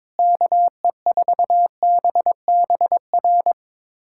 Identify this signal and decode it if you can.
Morse code, heard as a keyed tone, about 22 wpm, 700 Hz. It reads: KE4BBR
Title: USA Callsign DE Callsign - 6 Characters - 22wpm